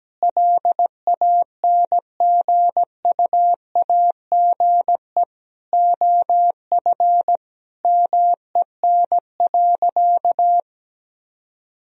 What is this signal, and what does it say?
Morse code, heard as a keyed tone, about 17 wpm, 700 Hz. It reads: LANGUAGE OF MEN.
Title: Sentences from Top 1000 Words - Review Part 3 - 17wpm